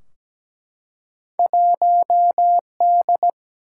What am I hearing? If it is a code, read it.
Morse code, heard as a keyed tone, about 17 wpm, 700 Hz. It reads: 1D
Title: Single-Letter Single-Number 17wpm